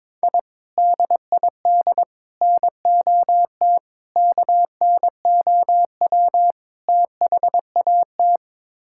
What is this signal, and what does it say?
Morse code, heard as a keyed tone, about 22 wpm, 700 Hz. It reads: I DID NOT KNOW THAT
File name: practice / 22wpm / Build A Phrase - 5-Word Phrases - 22wpm